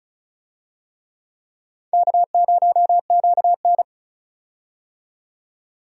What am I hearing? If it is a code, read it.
Morse code, heard as a keyed tone, about 36 wpm, 700 Hz. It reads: K0QN
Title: USA Call Signs - 4 Characters - 35wpm